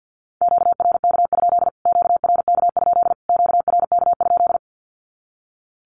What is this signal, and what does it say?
Morse code, heard as a keyed tone, about 48 wpm, 700 Hz. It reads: QRK? QRK? QRK?
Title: Q-Codes Repeated 3x 50wpm